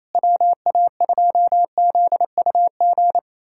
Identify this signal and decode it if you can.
Morse code, heard as a keyed tone, about 28 wpm, 700 Hz. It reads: WA2ZUG
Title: USA Callsign DE Callsign - 6 Characters - 28wpm